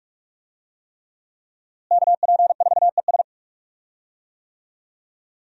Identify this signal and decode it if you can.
Morse code, heard as a keyed tone, about 46 wpm, 700 Hz. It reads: KP4ES
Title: USA Call Signs - 5 Characters - Repeated 3x - 45wpm